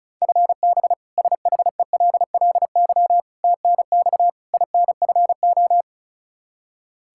Morse code, heard as a keyed tone, about 36 wpm, 700 Hz. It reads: FB SHELLY TNX INFO
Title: K1USN SST QSO Practice 35wpm